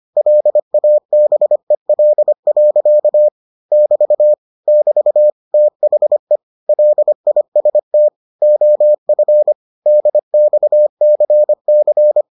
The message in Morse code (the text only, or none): LABEL. <BT> <BT> THE LIST OF DXCC